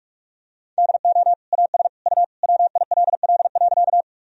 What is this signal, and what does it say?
Morse code, heard as a keyed tone, about 46 wpm, 700 Hz. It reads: DO AS U WILL.
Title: Sentences from Top 100 Words and Abbreviations 45wpm